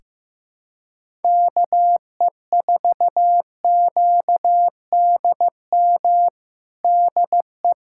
Morse code, heard as a keyed tone, about 16 wpm, 700 Hz. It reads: KE4QDM DE
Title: USA Callsign DE Callsign - 6 Characters - 15wpm